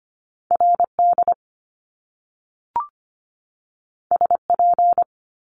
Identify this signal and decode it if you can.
Morse code, heard as a keyed tone, about 25 wpm, 700 Hz. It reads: RD SP
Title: Top Two Letter Combinations in Words 25wpm